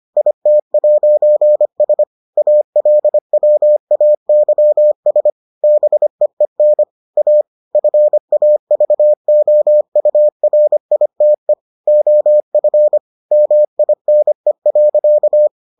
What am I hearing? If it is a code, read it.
Morse code, heard as a keyed tone, about 25 wpm, 600 Hz. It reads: IT'S ALWAYS BEEN A FAVOURITE OF MINE.